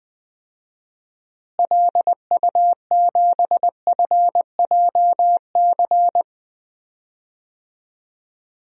Morse code, heard as a keyed tone, about 20 wpm, 700 Hz. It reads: LU7FJC
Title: DXCC-100 Call Signs - 6 Characters - Repeated 3x - 20wpm